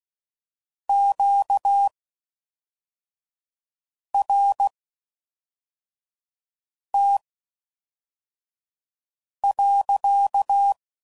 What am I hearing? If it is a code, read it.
Morse code, heard as a keyed tone, about 16 wpm, spaced as for 4 wpm, 800 Hz. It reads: QRT.